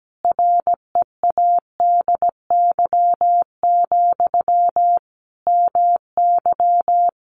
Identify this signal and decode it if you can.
Morse code, heard as a keyed tone, about 17 wpm, 700 Hz. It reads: READY, MY
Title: Sentences from Top 1300 Words 17wpm